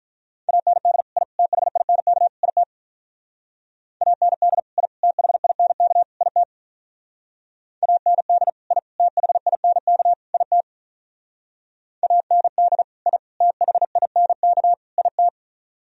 Morse code, heard as a keyed tone, about 42 wpm, 700 Hz. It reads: AND I THINK IT AND I THINK IT AND I THINK IT AND I THINK IT
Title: Speed-Racing - Sets of 4 Words 35wpm